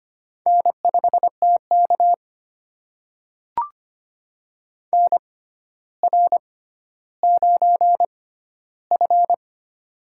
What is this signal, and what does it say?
Morse code, heard as a keyed tone, about 25 wpm, spaced as for 13 wpm, 700 Hz. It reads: N5TK NR9F
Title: USA Call Signs - 4 Characters - 6x Character Spacing - 25wpm